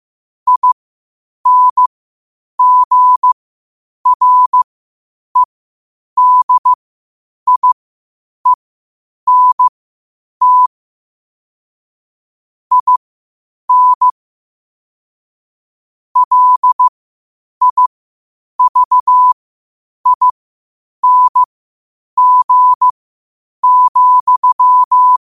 Morse code, heard as a keyed tone, about 15 wpm, spaced as for 8 wpm, 1000 Hz. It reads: INGREDIENT IN LIVING,